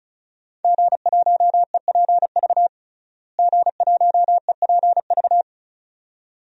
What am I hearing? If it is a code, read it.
Morse code, heard as a keyed tone, about 34 wpm, 700 Hz. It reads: G1EPV G1EPV
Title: DXCC-100 Call Signs - 5 Characters - Repeated 2x - 35wpm